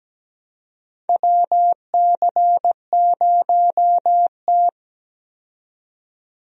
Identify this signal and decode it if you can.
Morse code, heard as a keyed tone, about 17 wpm, 700 Hz. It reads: WC0T
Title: USA Call Signs - 4 Characters - 17wpm